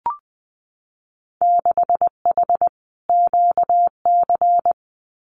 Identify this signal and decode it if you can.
Morse code, heard as a keyed tone, about 20 wpm, 700 Hz. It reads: 6H QC
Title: Winter Field Day Exchanges Encore 20wpm